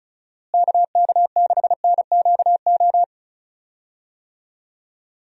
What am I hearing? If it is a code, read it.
Morse code, heard as a keyed tone, about 34 wpm, 700 Hz. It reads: KK6NQO
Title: USA Call Signs - 6 Characters - Repeated 2x - 35wpm